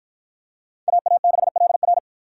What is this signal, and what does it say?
Morse code, heard as a keyed tone, about 53 wpm, 700 Hz. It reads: AA6LR